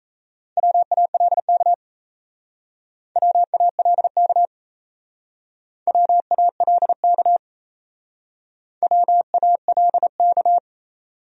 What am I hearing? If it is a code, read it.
Morse code, heard as a keyed tone, about 34 wpm, 700 Hz. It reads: WALK WALK WALK WALK
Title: Speed-Racing Top 300 Words 28wpm